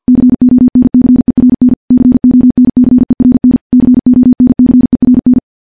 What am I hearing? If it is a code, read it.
Morse code, heard as a keyed tone, about 50 wpm, 250 Hz. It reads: CONCERN CONCERN CONCERN
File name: morse / morse-training-50WPM